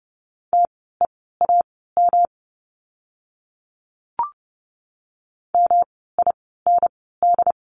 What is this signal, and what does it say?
Morse code, heard as a keyed tone, about 30 wpm, spaced as for 17 wpm, 700 Hz. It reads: TEAM MIND
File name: course / 30:10wpm / 108 Lesson 18f - Letter m - Words 30wpm